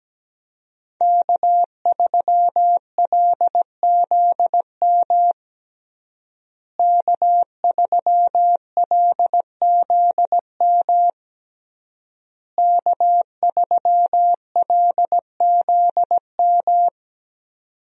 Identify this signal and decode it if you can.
Morse code, heard as a keyed tone, about 17 wpm, 700 Hz. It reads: K3LZM K3LZM K3LZM
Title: USA Call Signs - 5 Characters - Repeated 3x - 17wpm